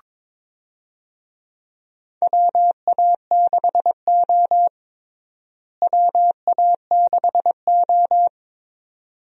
Morse code, heard as a keyed tone, about 22 wpm, 700 Hz. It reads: WA6O WA6O
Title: USA Call Signs - 4 Characters - Repeated 2x - 22wpm